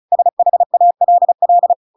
Morse code, heard as a keyed tone, about 34 wpm, 700 Hz. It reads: SHALL
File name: Pinocchio / Pinocchio ch7 35 35 ik1hge